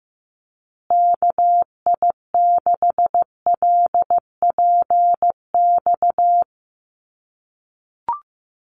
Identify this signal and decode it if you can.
Morse code, heard as a keyed tone, about 15 wpm, 700 Hz. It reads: KI6LPX
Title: USA Call Signs Repeated 3x 15wpm